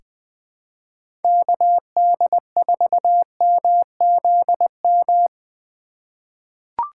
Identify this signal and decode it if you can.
Morse code, heard as a keyed tone, about 21 wpm, 700 Hz. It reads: KD4MZM
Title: 2023 POTA Activator Call Signs - 20wpm